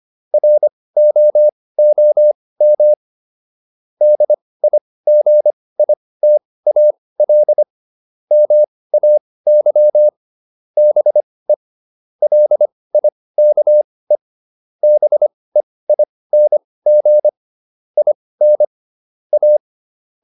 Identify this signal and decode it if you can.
Morse code, heard as a keyed tone, about 25 wpm, spaced as for 18 wpm, 600 Hz. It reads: ROOM DIGITAL MAY BE LIKE BEING IN A